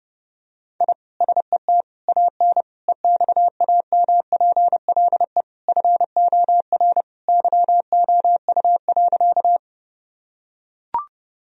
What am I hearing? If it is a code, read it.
Morse code, heard as a keyed tone, about 30 wpm, 700 Hz. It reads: I SET AN EXAMPLE FOR YOU.
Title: Sentences from Top 1000 Words - Review Part 1 - 30wpm